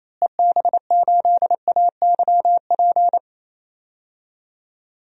Morse code, heard as a keyed tone, about 28 wpm, 700 Hz. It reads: EB8AYP
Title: DXCC-100 Call Signs - 6 Characters - Repeated 3x - 28wpm